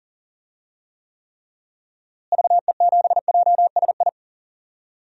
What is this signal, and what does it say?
Morse code, heard as a keyed tone, about 40 wpm, 700 Hz. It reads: VE7JSI
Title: DXCC-100 Call Signs - 6 Characters - Repeated 3x - 40wpm